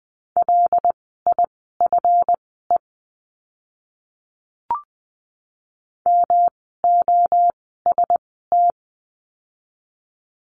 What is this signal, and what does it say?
Morse code, heard as a keyed tone, about 20 wpm, spaced as for 15 wpm, 700 Hz. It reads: LIFE MOST
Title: 237 Lesson 38f - Letter q - Words 20wpm